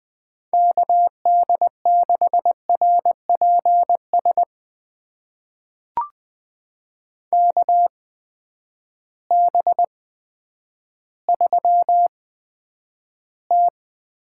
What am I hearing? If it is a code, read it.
Morse code, heard as a keyed tone, about 21 wpm, spaced as for 10 wpm, 700 Hz. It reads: KD6RPS KB3T
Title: USA Call Signs - 6 Characters - 8x Character Spacing - 20wpm